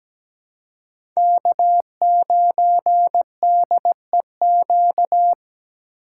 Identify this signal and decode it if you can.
Morse code, heard as a keyed tone, about 17 wpm, 700 Hz. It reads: K9DEQ
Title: USA Call Signs - 5 Characters - 2x Character Spacing - 17wpm